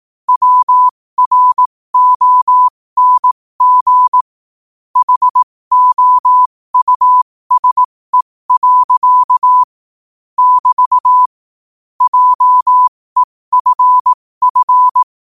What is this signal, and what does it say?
Morse code, heard as a keyed tone, about 18 wpm, spaced as for 15 wpm, 1000 Hz. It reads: WRONG HOUSE. <BT> JEFF
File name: 15wpm